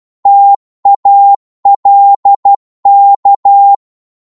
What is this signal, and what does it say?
Morse code, heard as a keyed tone, about 12 wpm, 800 Hz. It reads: TALK
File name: Morse / psg-12wpm